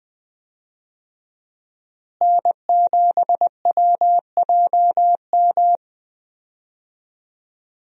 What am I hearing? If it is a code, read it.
Morse code, heard as a keyed tone, about 20 wpm, 700 Hz. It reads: N7WJM
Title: USA Call Signs - 5 Characters - Repeated 3x - 20wpm